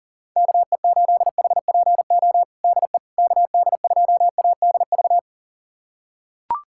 Morse code, heard as a keyed tone, about 37 wpm, 700 Hz. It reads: KE8HPO DE KD2ADV
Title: USA Callsign DE Callsign - 6 Characters - 40wpm